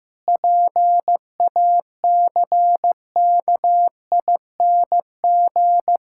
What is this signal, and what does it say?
Morse code, heard as a keyed tone, about 15 wpm, 700 Hz. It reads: PACKING
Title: Pangram Challenge v2 - 15wpm